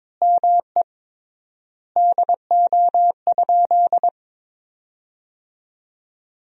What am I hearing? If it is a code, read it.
Morse code, heard as a keyed tone, about 22 wpm, 700 Hz. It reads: ME DO?
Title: 3x Word-Spacing - Sentences from Top 200 Words - 22wpm